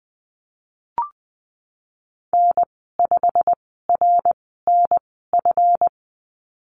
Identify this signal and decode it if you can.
Morse code, heard as a keyed tone, about 20 wpm, spaced as for 15 wpm, 700 Hz. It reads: N5RNF
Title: 161 Lesson 26g - Letter g - Call Signs 20wpm